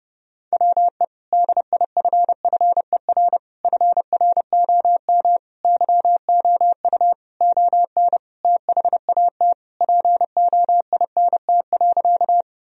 Morse code, heard as a keyed tone, about 30 wpm, 700 Hz. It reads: WE DIFFER FROM YOU ON THAT POINT.
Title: Sentences from Top 1000 Words - Review Part 1 - 30wpm